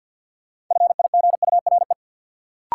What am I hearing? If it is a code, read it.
Morse code, heard as a keyed tone, about 53 wpm, 700 Hz. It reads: FIGURE
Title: Sets of 4 Words - New Word Drill - 50wpm